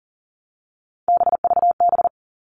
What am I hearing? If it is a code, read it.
Morse code, heard as a keyed tone, about 40 wpm, 700 Hz. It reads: BVB
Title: All Cut Numbers - 3-Digit 40wpm